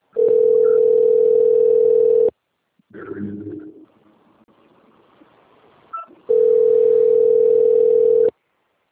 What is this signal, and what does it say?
Ringback tone (North America), 440 + 480 Hz.